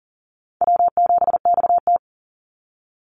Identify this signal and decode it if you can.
Morse code, heard as a keyed tone, about 40 wpm, 700 Hz. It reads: W7XT
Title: Call Signs Repeated 2x - 40wpm